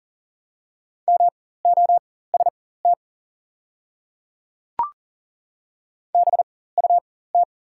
Morse code, heard as a keyed tone, about 40 wpm, spaced as for 19 wpm, 700 Hz. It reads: MOST BUT
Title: 4x Character Spacing - Top 200 Words - 40wpm